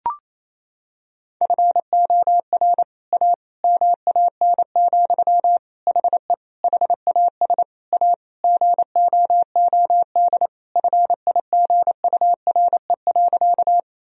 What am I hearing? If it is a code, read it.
Morse code, heard as a keyed tone, about 28 wpm, 700 Hz. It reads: FOR A MAN, HE HAS A GOOD FIGURE.
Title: Sentences from Top 1000 Words - Review Part 2 - 28wpm